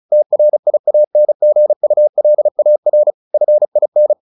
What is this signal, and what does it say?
Morse code, heard as a keyed tone, about 35 wpm, 600 Hz. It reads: TRIANGULAR FIN
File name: the old man and the sea 2 35wpm